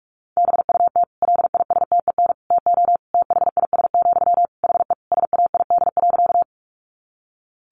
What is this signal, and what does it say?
Morse code, heard as a keyed tone, about 44 wpm, 700 Hz. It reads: BUT LISTEN TO THIS, HE SAID.
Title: Sentences from Top 400 Words 45wpm